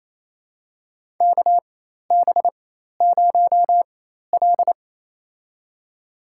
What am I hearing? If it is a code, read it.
Morse code, heard as a keyed tone, about 28 wpm, spaced as for 13 wpm, 700 Hz. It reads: KB0L
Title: USA Call Signs - 4 Characters - 4x Character Spacing - 28wpm